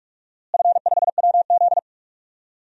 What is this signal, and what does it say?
Morse code, heard as a keyed tone, about 46 wpm, 700 Hz. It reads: F5WZ
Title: DXCC-100 Call Signs - 4 Characters - Repeated 3x - 45wpm